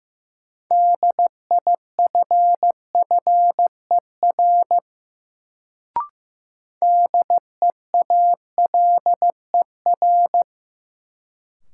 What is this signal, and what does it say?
Morse code, heard as a keyed tone, about 16 wpm, 700 Hz. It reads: DIFFER DEALER
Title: Sentences from Top 1100 Words - New Word Drill Alt - 15wpm